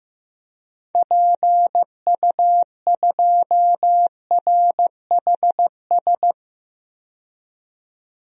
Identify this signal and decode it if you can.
Morse code, heard as a keyed tone, about 15 wpm, 700 Hz. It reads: PU2RHS
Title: DXCC-100 Call Signs - 6 Characters - 15wpm